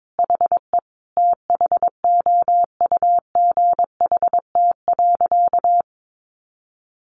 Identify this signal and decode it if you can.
Morse code, heard as a keyed tone, about 22 wpm, 700 Hz. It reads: HE THOUGHT.
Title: Sentences from Top 800 Words 22wpm